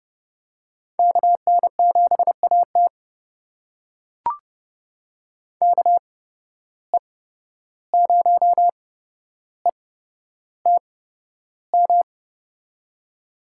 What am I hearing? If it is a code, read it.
Morse code, heard as a keyed tone, about 30 wpm, spaced as for 12 wpm, 700 Hz. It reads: KN7AT KE0ETM
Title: USA Call Signs - 8x Character Spacing - 30wpm